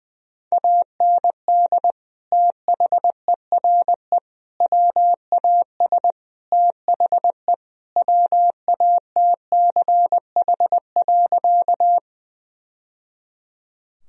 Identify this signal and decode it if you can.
Morse code, heard as a keyed tone, about 21 wpm, 700 Hz. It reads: AND THERE WAS THE WATCH.
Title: Sentences from Top 1000 Words - Review Part 1 - 20wpm